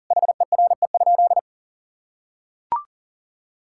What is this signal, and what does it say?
Morse code, heard as a keyed tone, about 40 wpm, 700 Hz. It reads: HERE?
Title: Sentences from Top 1000 Words - Review Part 6 - 40wpm